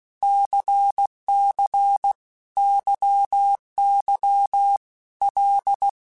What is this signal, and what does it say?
Morse code, heard as a keyed tone, about 16 wpm, 800 Hz. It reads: CC YY L
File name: Basic Letter Groups By7Groups / LFCYQ / LFCYQ@15